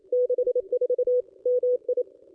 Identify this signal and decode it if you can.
Morse code, heard as a keyed tone, about 28 wpm, 500 Hz. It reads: 64 MI
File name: contests / ss-cw-k5tr-op-n5rz -1400